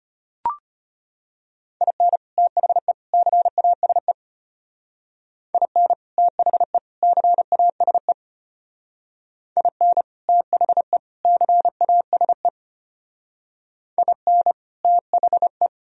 Morse code, heard as a keyed tone, about 30 wpm, 700 Hz. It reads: IN THE CASE IN THE CASE IN THE CASE IN THE
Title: Speed-Racing - Sets of 3 Words 25wpm